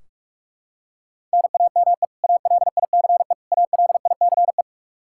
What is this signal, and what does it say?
Morse code, heard as a keyed tone, about 46 wpm, 700 Hz. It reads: NAME ALICE ALICE
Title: Practice QSO - Exchange 1 - Beginning - 45wpm